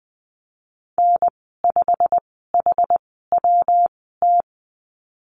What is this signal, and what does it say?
Morse code, heard as a keyed tone, about 20 wpm, 700 Hz. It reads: N 5 H W T
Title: 154 Lesson 25g - Letter p - Call Signs 20wpm